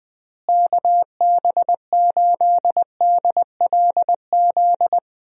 Morse code, heard as a keyed tone, about 20 wpm, 700 Hz. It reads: KB8DLZ
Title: USA Call Signs - 6 Characters - Repeated 3x - 20wpm